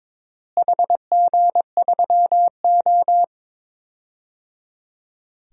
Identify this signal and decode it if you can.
Morse code, heard as a keyed tone, about 22 wpm, 700 Hz. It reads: HG3O
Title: DXCC-100 Call Signs - 4 Characters - Repeated 3x - 22wpm